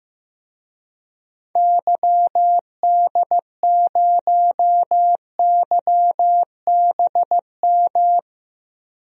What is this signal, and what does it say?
Morse code, heard as a keyed tone, about 15 wpm, 700 Hz. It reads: YD0YBM
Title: DXCC-100 Call Signs - 6 Characters - Repeated 3x - 15wpm